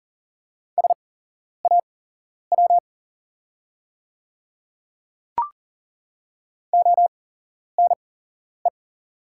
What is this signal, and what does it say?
Morse code, heard as a keyed tone, about 40 wpm, spaced as for 11 wpm, 690 Hz. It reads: SAW ONE
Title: 8x Character Spacing - Top 500 Words - 40wpm